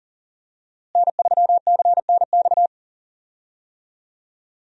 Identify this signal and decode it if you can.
Morse code, heard as a keyed tone, about 37 wpm, 700 Hz. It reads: N3CNX
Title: USA Call Signs - 5 Characters - Repeated 2x - 40wpm